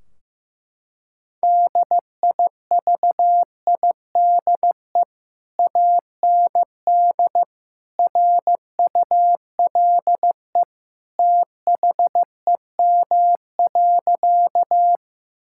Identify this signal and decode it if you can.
Morse code, heard as a keyed tone, about 14 wpm, 700 Hz. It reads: DIVIDE AND RULE THEM.
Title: Sentences from Top 1600 Words 15wpm